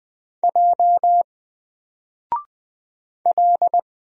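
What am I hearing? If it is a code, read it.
Morse code, heard as a keyed tone, about 20 wpm, 700 Hz. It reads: J L
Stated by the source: Single letter and number Morse drill at 20 wpm, from N4LCD.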